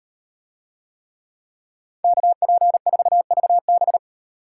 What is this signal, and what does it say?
Morse code, heard as a keyed tone, about 38 wpm, 700 Hz. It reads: KP4VB